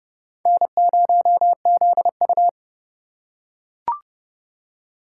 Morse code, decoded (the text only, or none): N0ZU